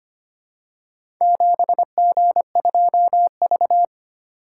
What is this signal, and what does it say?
Morse code, heard as a keyed tone, about 25 wpm, 700 Hz. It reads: 7G2V
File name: practice / 25wpm / Four Letter-Number Combinations - 25wpm